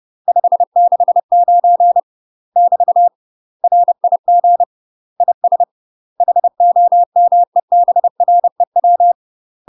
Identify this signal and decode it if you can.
Morse code, heard as a keyed tone, about 30 wpm, 700 Hz. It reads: 569 <BT> RIG IS HOMEBREW
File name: UK MASTER QSOs TRAINING / 30WPM / UK MASTER QSOs 0023